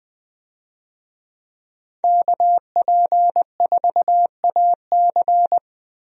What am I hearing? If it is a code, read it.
Morse code, heard as a keyed tone, about 20 wpm, 700 Hz. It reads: KP4AC
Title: USA Call Signs - 5 Characters - Repeated 3x - 20wpm